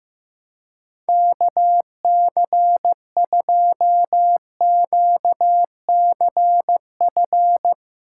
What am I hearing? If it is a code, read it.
Morse code, heard as a keyed tone, about 15 wpm, 700 Hz. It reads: KC2QCF
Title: USA Call Signs - 4x Character Spacing - 15wpm